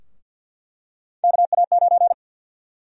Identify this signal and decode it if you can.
Morse code, heard as a keyed tone, about 46 wpm, 700 Hz. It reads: KA9
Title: USA Call Sign Prefixes - Repeated 2x - 50wpm